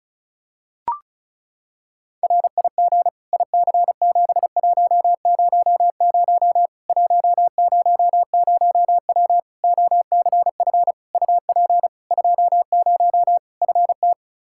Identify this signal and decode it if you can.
Morse code, heard as a keyed tone, about 34 wpm, 700 Hz. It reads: RIG IC7100 100W OCF UP 20 FT
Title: Practice QSO - Exchange 2 - Info - 35wpm